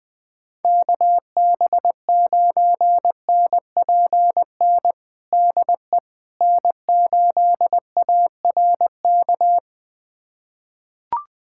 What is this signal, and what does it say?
Morse code, heard as a keyed tone, about 20 wpm, 700 Hz. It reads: KB9NPN DE N8ARK
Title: USA Callsign DE Callsign 20wpm